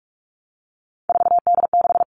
Morse code, heard as a keyed tone, about 44 wpm, 700 Hz. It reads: 4D6
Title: All Cut Numbers - 3-Digit 45wpm